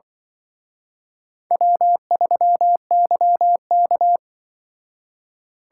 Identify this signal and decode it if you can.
Morse code, heard as a keyed tone, about 24 wpm, 700 Hz. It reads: W3YK
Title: Speed Racing - USA Call Signs 20wpm